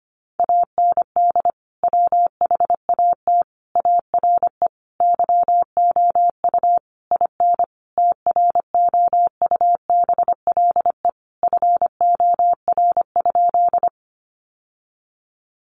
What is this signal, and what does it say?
Morse code, heard as a keyed tone, about 25 wpm, 700 Hz. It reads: AND WHAT ARE YOU IN TROUBLE FOR?
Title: Sentences from Top 1200 Words 25wpm